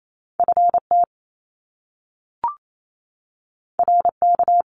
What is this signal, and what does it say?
Morse code, heard as a keyed tone, about 28 wpm, 700 Hz. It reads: FT RK
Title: Top Two Letter Combinations in Words 28wpm